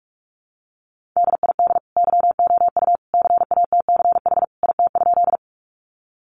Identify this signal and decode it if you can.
Morse code, heard as a keyed tone, about 44 wpm, 700 Hz. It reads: DID YOU CATCH IT?
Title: Sentences from Top 1500 Words 45wpm